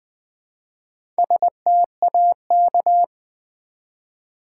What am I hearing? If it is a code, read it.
Morse code, heard as a keyed tone, about 20 wpm, 700 Hz. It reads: STAK
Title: Build A Word - 1 Letter At A Time - 5-Letter Words - 20wpm